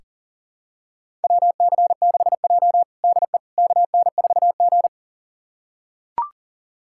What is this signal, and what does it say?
Morse code, heard as a keyed tone, about 37 wpm, 700 Hz. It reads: WC6J DE KN4G
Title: USA Callsign DE Callsign - 4 Characters - 40wpm